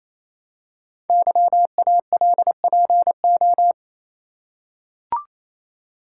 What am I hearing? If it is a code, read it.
Morse code, heard as a keyed tone, about 28 wpm, 700 Hz. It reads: YALPO
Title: Code Groups - 5 Letters - 28wpm